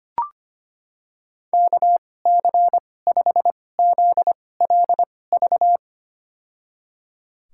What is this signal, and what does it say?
Morse code, heard as a keyed tone, about 25 wpm, spaced as for 18 wpm, 700 Hz. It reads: KC5ZLV
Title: USA Call Signs - 6 Characters - 2x Character Spacing - 25wpm